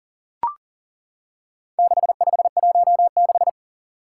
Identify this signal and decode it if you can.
Morse code, heard as a keyed tone, about 40 wpm, 700 Hz. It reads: 6516
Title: Four Challenging Letter-Number Combinations 40wpm